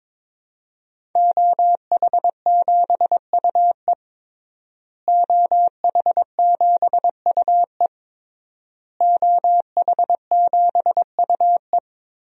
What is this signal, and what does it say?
Morse code, heard as a keyed tone, about 22 wpm, 700 Hz. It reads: OH7UE OH7UE OH7UE
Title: DXCC-100 Call Signs - 5 Characters - Repeated 3x - 22wpm